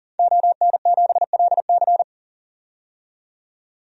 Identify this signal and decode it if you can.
Morse code, heard as a keyed tone, about 40 wpm, 700 Hz. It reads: ON7LC
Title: DXCC-100 Call Signs - 5 Characters - 40wpm